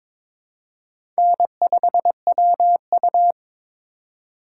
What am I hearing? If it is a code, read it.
Morse code, heard as a keyed tone, about 22 wpm, 690 Hz. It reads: N5WU
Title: USA Call Signs - 4 Characters - 3x Character Spacing - 22wpm